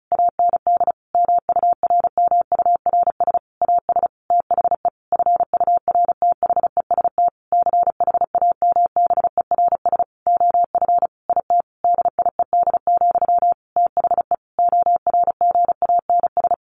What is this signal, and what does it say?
Morse code, heard as a keyed tone, about 34 wpm, 700 Hz. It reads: AND MURMURS AS THE FURTHEST CHAMBERS OF IT DIED, THE ORGANS
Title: A Sound Of Thunder 35wpm